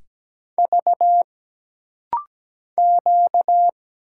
Morse code, heard as a keyed tone, about 17 wpm, 700 Hz. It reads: V Q
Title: Single Letters - Rapid-Fire 17wpm